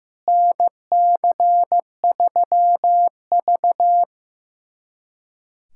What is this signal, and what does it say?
Morse code, heard as a keyed tone, about 16 wpm, 700 Hz. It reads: NC3V
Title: USA Callsign DE Callsign - 4 Characters - 15wpm